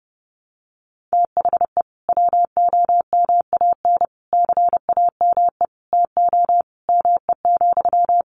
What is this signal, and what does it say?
Morse code, heard as a keyed tone, about 30 wpm, 700 Hz. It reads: THE WOMAN CAME TO ME,
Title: Sentences from Top 1100 Words 30wpm